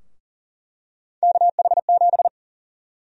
Morse code, heard as a keyed tone, about 40 wpm, 700 Hz. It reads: KH7
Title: USA Call Sign Prefixes - Repeated 2x - 40wpm